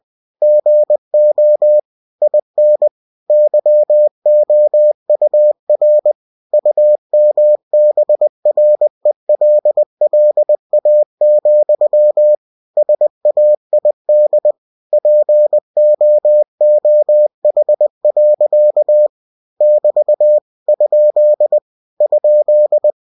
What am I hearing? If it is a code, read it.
Morse code, heard as a keyed tone, about 20 wpm, 600 Hz. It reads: GO IN YOUR UMBRELLA, SAID POOH. <BT> ? ?